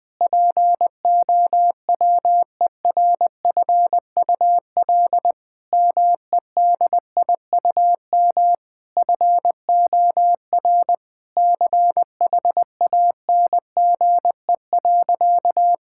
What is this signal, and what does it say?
Morse code, heard as a keyed tone, about 20 wpm, 700 Hz. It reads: POWERFUL MEDIUM FOR CHANGE.
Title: Sentences from Top 1000 Words - Review Part 6 - 20wpm